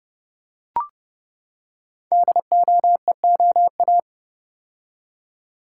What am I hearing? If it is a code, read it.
Morse code, heard as a keyed tone, about 30 wpm, 700 Hz. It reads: DOEOA
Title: Code Groups - 5 Letters - 30wpm